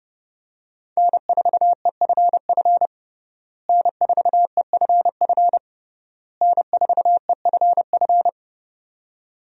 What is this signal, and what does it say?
Morse code, heard as a keyed tone, about 30 wpm, 700 Hz. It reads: N4EFF N4EFF N4EFF
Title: USA Call Signs - 5 Characters - Repeated 3x - 30wpm